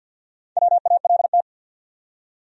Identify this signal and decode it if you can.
Morse code, heard as a keyed tone, about 47 wpm, 700 Hz. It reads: WALT
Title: USA Common Names - Max 4 Letters - 50wpm